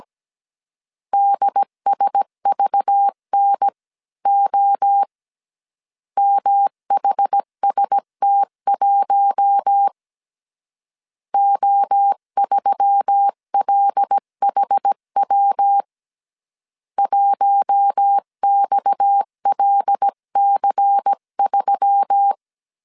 Morse code, heard as a keyed tone, about 17 wpm, 800 Hz. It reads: DSVNO MHST1 O3LHW 1XLC3
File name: CW / DET / 1 6 1500C S C 16wpm 2733